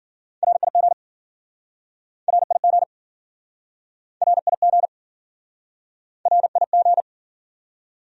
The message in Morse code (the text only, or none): RIG RIG RIG RIG